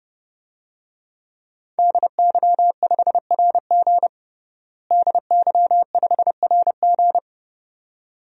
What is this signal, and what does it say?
Morse code, heard as a keyed tone, about 30 wpm, 700 Hz. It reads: DY5RG DY5RG
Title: DXCC-100 Call Signs - 5 Characters - Repeated 2x - 30wpm